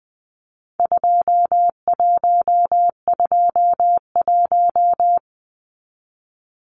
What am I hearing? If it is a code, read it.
Morse code, heard as a keyed tone, about 20 wpm, 700 Hz. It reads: 2121
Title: Numbers - 4-Digit 20wpm